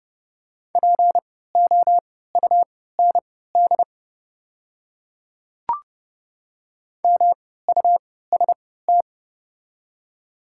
Morse code, heard as a keyed tone, about 30 wpm, spaced as for 17 wpm, 700 Hz. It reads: POUND MUST
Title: 3x Character Spacing - Top 500 Words - 30wpm